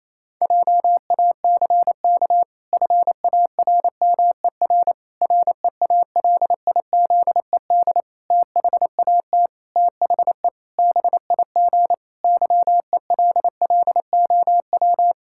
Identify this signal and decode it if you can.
Morse code, heard as a keyed tone, about 27 wpm, 700 Hz. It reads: JACK FARMER REALIZED THAT THE BIG YELLOW
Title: Pangram Challenge v2 - 28wpm